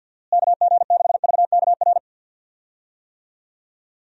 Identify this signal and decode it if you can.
Morse code, heard as a keyed tone, about 51 wpm, 700 Hz. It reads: KG6VKR
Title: USA Call Signs - 6 Characters - Repeated 3x - 50wpm